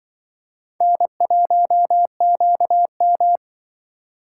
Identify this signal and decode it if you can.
Morse code, heard as a keyed tone, about 25 wpm, 700 Hz. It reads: N1QM
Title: Speed Racing - USA 1x2 Call Signs 20wpm